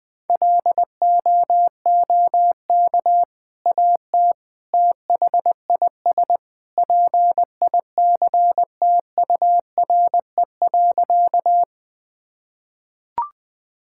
Morse code, heard as a keyed tone, about 20 wpm, 700 Hz. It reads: LOOK AT THIS PICTURE.
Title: Sentences with 2-Word Phrases - 20wpm